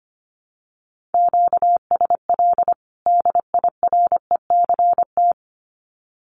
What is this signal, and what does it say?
Morse code, heard as a keyed tone, about 25 wpm, 700 Hz. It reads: QSL DIRECT
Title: Common QSO Phrases 25wpm